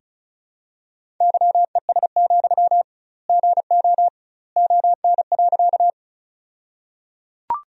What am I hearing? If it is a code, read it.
Morse code, heard as a keyed tone, about 34 wpm, 700 Hz. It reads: YES, GO ON.
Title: 2x Word-Spacing - Sentences from Top 500 Words - 35wpm